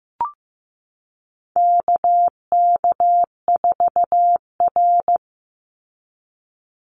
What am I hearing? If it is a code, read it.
Morse code, heard as a keyed tone, about 15 wpm, 700 Hz. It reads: KK4R
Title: Call Signs Repeated 2x - 15wpm